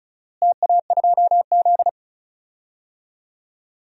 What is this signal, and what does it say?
Morse code, heard as a keyed tone, about 36 wpm, 700 Hz. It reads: TA2Z
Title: DXCC-100 Call Signs - 4 Characters - Repeated 3x - 35wpm